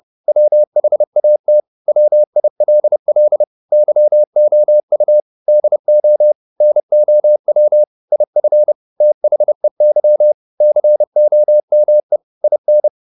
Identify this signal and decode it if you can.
Morse code, heard as a keyed tone, about 30 wpm, 600 Hz. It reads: WHAT WILL YOU DO NOW IF THEY COME IN